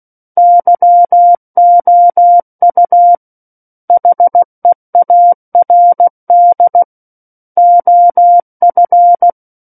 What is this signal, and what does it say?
Morse code, heard as a keyed tone, about 16 wpm, 700 Hz. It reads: YOU HEARD OF
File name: Book One / 07-Chapter 7 - 16 WPM